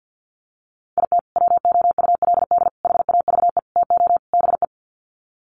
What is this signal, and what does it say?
Morse code, heard as a keyed tone, about 48 wpm, 700 Hz. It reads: IT WOULD HAVE TO BE
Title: Sets of 5 Words Encore 50wpm